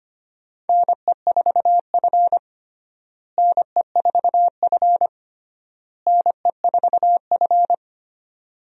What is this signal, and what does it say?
Morse code, heard as a keyed tone, about 25 wpm, 700 Hz. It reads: NE4F NE4F NE4F
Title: USA Call Signs - 4 Characters - Repeated 3x - 25wpm